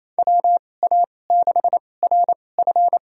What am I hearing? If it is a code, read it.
Morse code, heard as a keyed tone, about 28 wpm, spaced as for 20 wpm, 700 Hz. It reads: WA6RF
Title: USA Call Signs - 2x Character Spacing - 28wpm